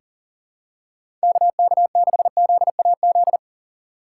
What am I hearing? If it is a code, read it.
Morse code, heard as a keyed tone, about 40 wpm, 700 Hz. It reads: KK6ZAZ
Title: USA Call Signs - 6 Characters - 6x Character Spacing - 40wpm